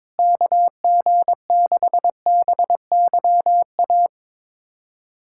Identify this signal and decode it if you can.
Morse code, heard as a keyed tone, about 22 wpm, 700 Hz. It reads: KG6BYA
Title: USA Callsign DE Callsign 22wpm